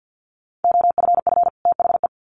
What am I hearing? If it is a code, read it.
Morse code, heard as a keyed tone, about 50 wpm, 700 Hz. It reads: OFF THE
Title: Sets of 2 Words 50wpm